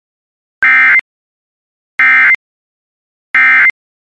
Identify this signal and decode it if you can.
Emergency Alert System (SAME) tones, heard as harsh buzzing bursts. It is heard from 0.6 to 3.7 s.